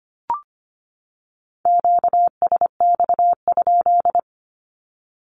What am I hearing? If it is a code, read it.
Morse code, heard as a keyed tone, about 25 wpm, 700 Hz. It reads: QSX?
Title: Q-Codes 25wpm